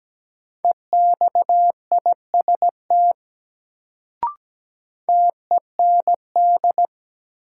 Morse code, heard as a keyed tone, about 17 wpm, 700 Hz. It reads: EXIST TEND
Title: Sentences from Top 1000 Words - Review Part 5 - New Word Drill - 17wpm